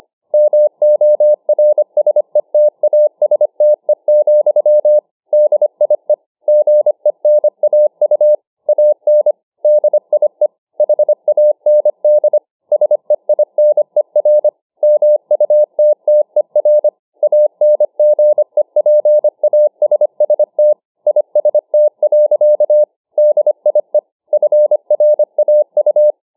Morse code, heard as a keyed tone, about 25 wpm, 600 Hz. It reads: MORSETASTE, DIE GENAU AN DIE HAND SEINER MUTTER ANGEPASST IST. DIE FRAU